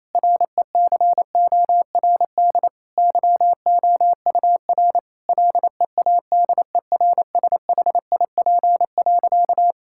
Morse code, heard as a keyed tone, about 28 wpm, 700 Hz. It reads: RECORD YOUR LEADERSHIP.
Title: Sentences from Top 1600 Words 28wpm